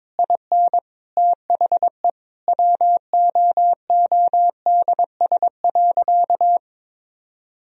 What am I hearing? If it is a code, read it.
Morse code, heard as a keyed tone, about 22 wpm, 700 Hz. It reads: IN THE WOODS.
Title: Sentences from Top 1000 Words - Review Part 2 - 22wpm